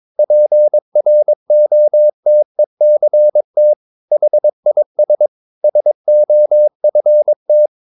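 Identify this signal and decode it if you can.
Morse code, heard as a keyed tone, about 22 wpm, 600 Hz. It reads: PROTECT HIS SOFT
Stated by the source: Morse code practice at 22 wpm, sending text from an English novel.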